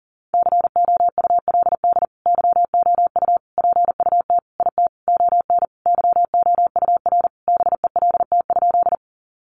Morse code, heard as a keyed tone, about 40 wpm, 700 Hz. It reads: COULD YOU PUT IT ON YOUR BELT?
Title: Sentences from Top 1400 Words 40wpm